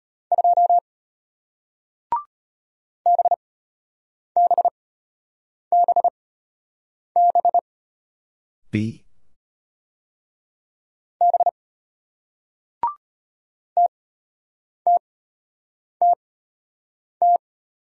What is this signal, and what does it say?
Morse code, heard as a keyed tone, about 33 wpm, spaced as for 8 wpm, 700 Hz. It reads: 2 BBBB B TTTT